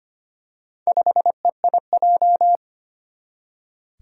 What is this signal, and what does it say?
Morse code, heard as a keyed tone, about 25 wpm, 700 Hz. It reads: 5EIJ
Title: Four Challenging Letter-Number Combinations 25wpm